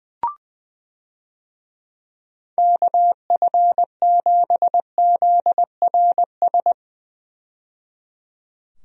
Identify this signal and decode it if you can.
Morse code, heard as a keyed tone, about 21 wpm, 700 Hz. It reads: KF7ZRS
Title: USA Call Signs - 6 Characters - 20wpm